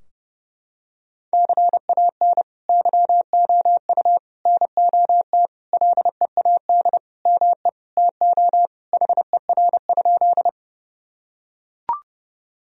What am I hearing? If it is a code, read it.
Morse code, heard as a keyed tone, about 32 wpm, 700 Hz. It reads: CAN YOU NOT LEAD ME TO HER?
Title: Sentences from Top 1000 Words - Review Part 2 - 30wpm